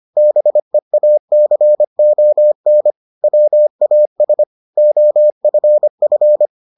Morse code, heard as a keyed tone, about 25 wpm, 600 Hz. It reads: BEACON WAS OFF